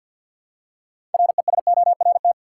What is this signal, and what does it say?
Morse code, heard as a keyed tone, about 50 wpm, 700 Hz. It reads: RESORT
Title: Six Letter Words - 50wpm